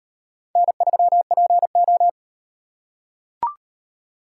Morse code, heard as a keyed tone, about 35 wpm, 700 Hz. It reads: N3PO